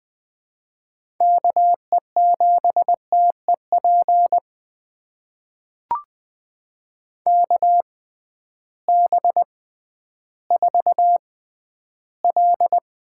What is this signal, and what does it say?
Morse code, heard as a keyed tone, about 20 wpm, spaced as for 12 wpm, 700 Hz. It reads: KE7TEP KB4L
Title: USA Call Signs - 6 Characters - 6x Character Spacing - 20wpm